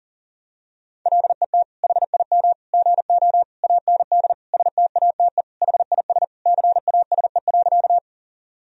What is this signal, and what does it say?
Morse code, heard as a keyed tone, about 40 wpm, 700 Hz. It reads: LET HIM GO AND STATE HIS CASE.
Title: Sentences from Top 1000 Words - Review Part 2 - 40wpm